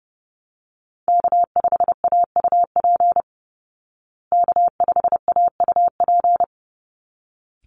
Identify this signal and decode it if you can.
Morse code, heard as a keyed tone, about 28 wpm, 700 Hz. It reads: K5AUP K5AUP
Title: Call Signs Repeated 3x - 30wpm